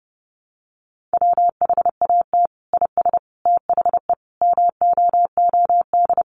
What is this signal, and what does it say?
Morse code, heard as a keyed tone, about 28 wpm, 700 Hz. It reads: WHAT IS THE MOOD
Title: Sentences from Top 1100 Words 30wpm